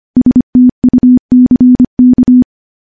Morse code, heard as a keyed tone, about 25 wpm, 250 Hz. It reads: STUCK
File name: morse-training-25WPM